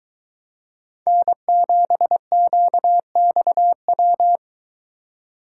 Morse code, heard as a keyed tone, about 22 wpm, 700 Hz. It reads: N7QXW